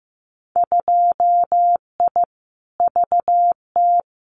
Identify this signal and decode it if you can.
Morse code, heard as a keyed tone, about 15 wpm, 700 Hz. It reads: 2I VT
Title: Winter Field Day Exchanges 15wpm